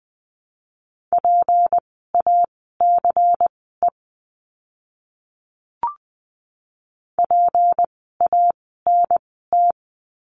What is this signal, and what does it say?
Morse code, heard as a keyed tone, about 20 wpm, spaced as for 15 wpm, 700 Hz. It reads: PACE PANT
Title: 153 Lesson 25f - Letter p - Words 20wpm